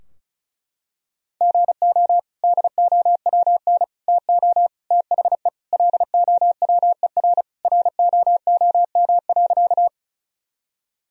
Morse code, heard as a keyed tone, about 36 wpm, 700 Hz. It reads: GO DOWN TO THE LOWER ROOM.
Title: Sentences from Top 1000 Words - Review Part 4 - 35wpm